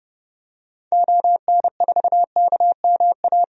Morse code, heard as a keyed tone, about 32 wpm, 700 Hz. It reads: ON4KMA
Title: DXCC-100 Call Signs - 6 Characters - 30wpm